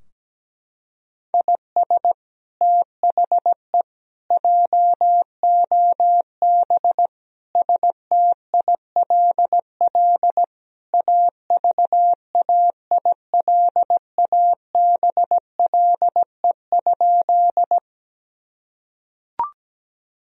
Morse code, heard as a keyed tone, about 17 wpm, 700 Hz. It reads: IS THE JOB STILL AVAILABLE?
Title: Sentences from Top 1000 Words - Review Part 3 - 17wpm